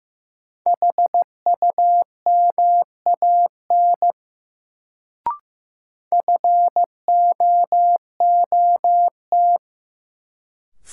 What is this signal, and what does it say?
Morse code, heard as a keyed tone, about 16 wpm, 700 Hz. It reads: HUMAN FOOT
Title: Sets of 5 Words - New Word Drill - 15wpm